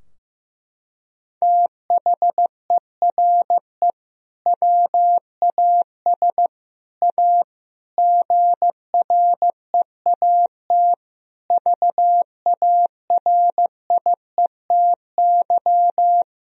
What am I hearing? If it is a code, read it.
Morse code, heard as a keyed tone, about 16 wpm, 700 Hz. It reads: THERE WAS A GREAT VARIETY
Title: Sentences from Top 1000 Words - Review Part 4 - 15wpm